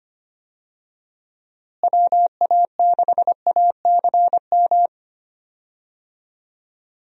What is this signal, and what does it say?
Morse code, heard as a keyed tone, about 25 wpm, 700 Hz. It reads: WA6ACM
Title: USA Call Signs - 6 Characters - Repeated 3x - 25wpm